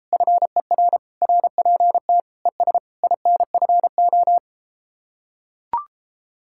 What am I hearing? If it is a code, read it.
Morse code, heard as a keyed tone, about 33 wpm, 700 Hz. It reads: FER RPT ES INFO